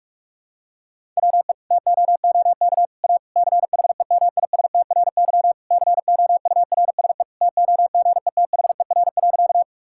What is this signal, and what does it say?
Morse code, heard as a keyed tone, about 46 wpm, 700 Hz. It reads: WE TOOK A CHEMISTRY COURSE TOGETHER.
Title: Sentences from Top 1000 Words - Review Part 6 - 45wpm